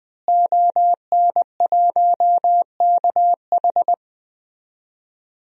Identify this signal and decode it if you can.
Morse code, heard as a keyed tone, about 20 wpm, 700 Hz. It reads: ON1KH
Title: DXCC-100 Call Signs - 5 Characters - Repeated 3x - 20wpm